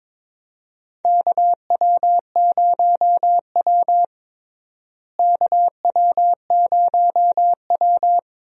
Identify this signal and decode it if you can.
Morse code, heard as a keyed tone, about 22 wpm, 700 Hz. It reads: KW0W KW0W
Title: USA Call Signs - 4 Characters - Repeated 2x - 22wpm